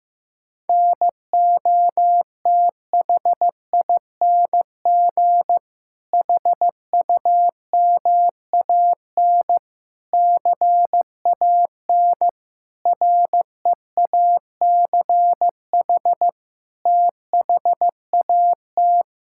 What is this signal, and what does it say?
Morse code, heard as a keyed tone, about 16 wpm, 700 Hz. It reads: NOTHING HUMAN CAN REACH THAT
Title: Sentences from Top 1600 Words 15wpm